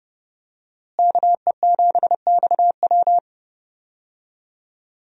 Morse code, heard as a keyed tone, about 30 wpm, 700 Hz. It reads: KE7XW
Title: USA Call Signs - 5 Characters - Repeated 2x - 30wpm